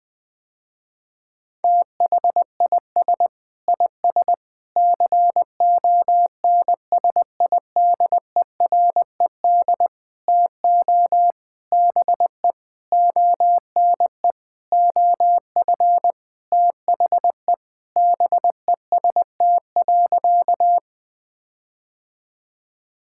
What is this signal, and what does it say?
Morse code, heard as a keyed tone, about 21 wpm, 700 Hz. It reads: THIS IS CONSIDERED TO BE ONE OF THE BEST.
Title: Sentences with 7-Word Phrases - Spoken Beginning - 20wpm